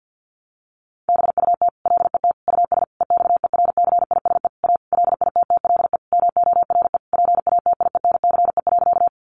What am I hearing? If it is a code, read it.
Morse code, heard as a keyed tone, about 50 wpm, 700 Hz. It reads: BUT LET US EXERCISE A LITTLE MORE PATIENCE.
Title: Sentences from Top 1300 Words 50wpm